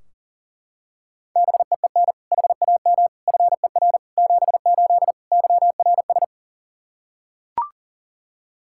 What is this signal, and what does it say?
Morse code, heard as a keyed tone, about 40 wpm, 700 Hz. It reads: BEEN HAM FER 78 YRS
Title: K1USN SST QSO Practice 40wpm